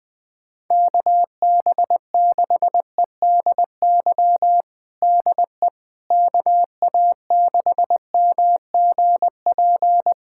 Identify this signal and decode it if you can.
Morse code, heard as a keyed tone, about 21 wpm, 700 Hz. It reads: KB6EDY DE KA6MGP
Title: USA Callsign DE Callsign - 6 Characters - 20wpm